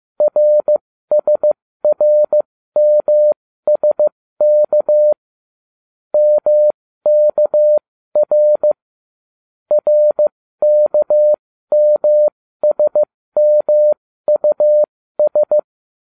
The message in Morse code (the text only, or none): RSRMSK MKR RKMSMUS